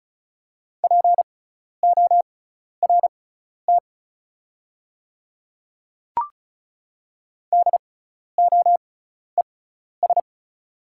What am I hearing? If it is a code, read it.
Morse code, heard as a keyed tone, about 34 wpm, spaced as for 12 wpm, 700 Hz. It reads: PORT DOES
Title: 6x Character Spacing - Top 200 Words - 35wpm